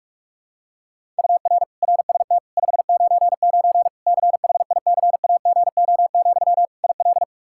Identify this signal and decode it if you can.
Morse code, heard as a keyed tone, about 46 wpm, 700 Hz. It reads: UR RST 599 CHICAGO, IL
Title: Practice QSO - Exchange 1 - Beginning - 45wpm